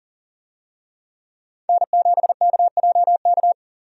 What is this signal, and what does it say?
Morse code, heard as a keyed tone, about 40 wpm, 700 Hz. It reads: N7KJK
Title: USA Call Signs - 5 Characters - 40wpm